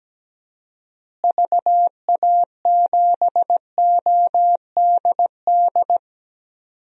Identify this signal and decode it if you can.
Morse code, heard as a keyed tone, about 17 wpm, 700 Hz. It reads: VA7ODD
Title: DXCC-100 Call Signs - 6 Characters - Repeated 3x - 17wpm